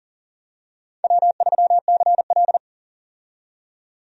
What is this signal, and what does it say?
Morse code, heard as a keyed tone, about 40 wpm, 700 Hz. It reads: W3CL
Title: USA Call Signs - 4 Characters - 40wpm